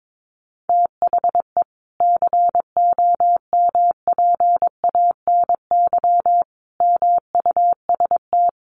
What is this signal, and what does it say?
Morse code, heard as a keyed tone, about 22 wpm, 700 Hz. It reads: THE COMPANY MUST
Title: Sentences from Top 1400 Words 22wpm